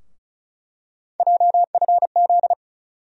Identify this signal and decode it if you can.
Morse code, heard as a keyed tone, about 34 wpm, 700 Hz. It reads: JFZ
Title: USA Call Sign Postfixes - 3 Letters 35wpm